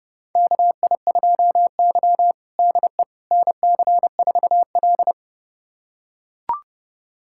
Morse code, heard as a keyed tone, about 32 wpm, 700 Hz. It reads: KI2Y DE NC4L
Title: USA Callsign DE Callsign - 4 Characters - 30wpm